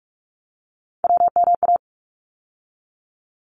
Morse code, heard as a keyed tone, about 44 wpm, 700 Hz. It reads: WMA
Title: ARRL Sections 45wpm